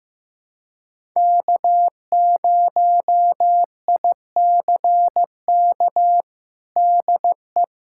Morse code, heard as a keyed tone, about 15 wpm, 700 Hz. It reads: K0ICK DE
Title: USA Callsign DE Callsign - 5 Characters - 15wpm